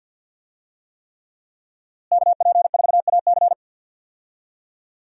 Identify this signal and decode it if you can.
Morse code, heard as a keyed tone, about 52 wpm, 700 Hz. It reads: KP4AC
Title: USA Call Signs - 5 Characters - Repeated 3x - 50wpm